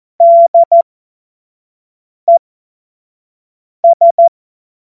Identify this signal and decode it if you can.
Morse code, heard as a keyed tone, about 14 wpm, spaced as for 5 wpm, 680 Hz. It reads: DES
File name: Morsekurs / Witze / Witz 18. Juli 2023 25 F70